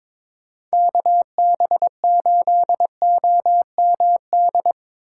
Morse code, heard as a keyed tone, about 25 wpm, 700 Hz. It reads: KB8OMD
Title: Speed Racing - USA 2x3 Call Signs 22wpm